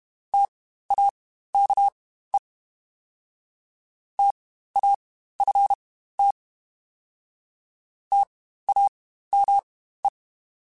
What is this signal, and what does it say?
Morse code, heard as a keyed tone, about 32 wpm, spaced as for 15 wpm, 800 Hz. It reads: TAKE TAFT TAME
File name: Spaced Alphas / Spaced Alphas mp3s / Ts sp @30